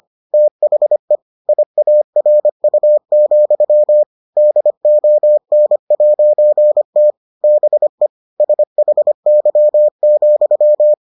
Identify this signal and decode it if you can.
Morse code, heard as a keyed tone, about 25 wpm, 600 Hz. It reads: THE IARU, DON'T BE SHY,